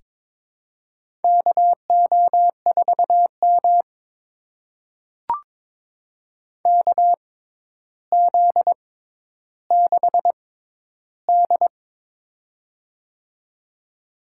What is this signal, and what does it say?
Morse code, heard as a keyed tone, about 22 wpm, spaced as for 11 wpm, 700 Hz. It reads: KO4M KZ6D
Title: USA Call Signs - 4 Characters - 6x Character Spacing - 22wpm